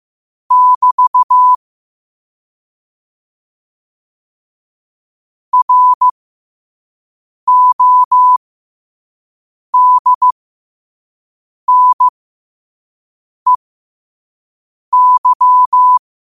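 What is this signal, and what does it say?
Morse code, heard as a keyed tone, about 15 wpm, spaced as for 5 wpm, 1000 Hz. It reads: <BT> RODNEY
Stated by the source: Morse code sent at 5 wpm.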